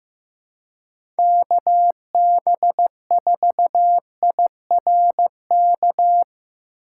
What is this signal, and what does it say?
Morse code, heard as a keyed tone, about 15 wpm, 700 Hz. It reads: KB4IRK
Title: USA Call Signs - 6 Characters - 4x Character Spacing - 15wpm